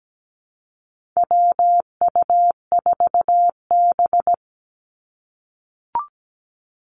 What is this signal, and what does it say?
Morse code, heard as a keyed tone, about 17 wpm, 700 Hz. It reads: WU4B
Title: Call Signs - 17wpm